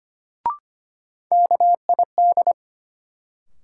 Morse code, heard as a keyed tone, about 25 wpm, 700 Hz. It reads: KID
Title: Sentences from Top 1600 Words - New Word Drill Alt - 25wpm